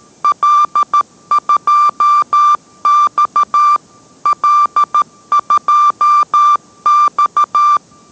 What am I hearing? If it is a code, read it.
Morse code, heard as a keyed tone, about 13 wpm, 1180 Hz. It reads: L2X L2X